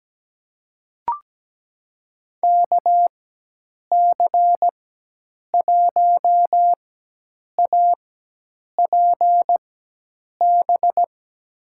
Morse code, heard as a keyed tone, about 17 wpm, spaced as for 8 wpm, 700 Hz. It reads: KC1APB
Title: USA Call Signs - 6 Characters - 4x Character Spacing - 17wpm